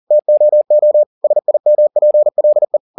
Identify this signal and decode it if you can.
Morse code, heard as a keyed tone, about 40 wpm, 600 Hz. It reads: TOO SIMPLE